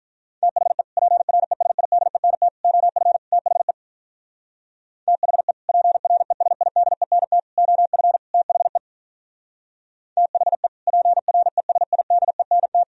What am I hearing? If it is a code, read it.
Morse code, heard as a keyed tone, about 45 wpm, 700 Hz. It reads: THE PRESIDENT OF THE THE PRESIDENT OF THE THE PRESIDENT